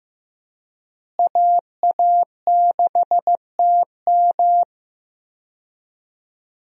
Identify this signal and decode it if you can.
Morse code, heard as a keyed tone, about 16 wpm, 700 Hz. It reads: AA6TM
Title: USA Call Signs - 5 Characters - Repeated 3x - 15wpm